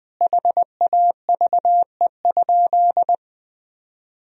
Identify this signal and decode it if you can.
Morse code, heard as a keyed tone, about 21 wpm, 700 Hz. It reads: HAVE?
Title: Sentences from Top 1000 Words - Review Part 1 - 20wpm